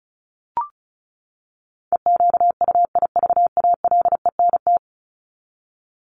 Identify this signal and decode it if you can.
Morse code, heard as a keyed tone, about 34 wpm, 700 Hz. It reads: EQUIVALENT
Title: Top 1501-1750 Words 35wpm